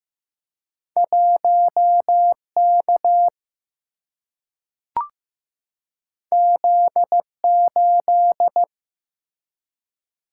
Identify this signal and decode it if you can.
Morse code, heard as a keyed tone, about 15 wpm, 700 Hz. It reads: 1K Z8
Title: Single-Letter Single-Number 15wpm